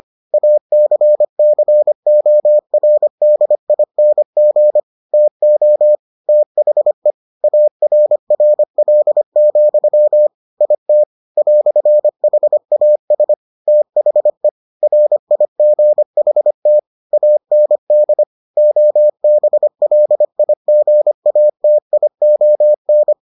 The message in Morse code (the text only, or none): ACCORDING TO THE ARRL, IT "HAS THE RIGHT AND OBLIGATION